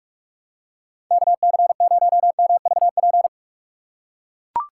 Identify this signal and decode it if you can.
Morse code, heard as a keyed tone, about 44 wpm, 700 Hz. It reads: KC0MVP